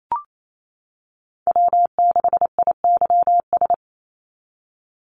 Morse code, heard as a keyed tone, about 28 wpm, 700 Hz. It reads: W6IYS
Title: NA SOTA Chaser Callsigns 28wpm